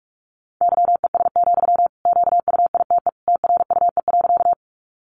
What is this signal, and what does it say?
Morse code, heard as a keyed tone, about 44 wpm, 700 Hz. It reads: YES, QUITE TRUE.
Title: Sentences from Top 700 Words 45wpm